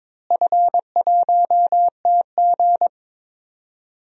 Morse code, heard as a keyed tone, about 22 wpm, 700 Hz. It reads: F1TG
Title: DXCC-100 Call Signs - 4 Characters - Repeated 2x - 22wpm